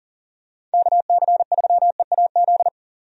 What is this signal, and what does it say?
Morse code, heard as a keyed tone, about 40 wpm, 700 Hz. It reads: KC3EAZ
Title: USA Call Signs - 6 Characters - 40wpm